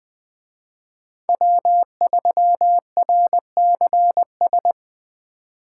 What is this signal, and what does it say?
Morse code, heard as a keyed tone, about 20 wpm, 700 Hz. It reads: W3RCS
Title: Speed Racing - USA 1x3 Call Signs 17wpm